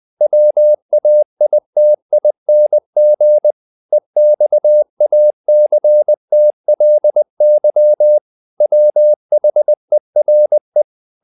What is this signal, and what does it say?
Morse code, heard as a keyed tone, about 20 wpm, 600 Hz. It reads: WAITING EXACTLY WHERE
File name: the old man and the sea / the old man and the sea part 1 20wpm